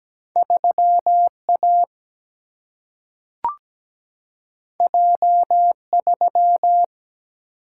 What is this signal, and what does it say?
Morse code, heard as a keyed tone, about 17 wpm, 700 Hz. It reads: 3A J3
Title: Single-Letter Single-Number 17wpm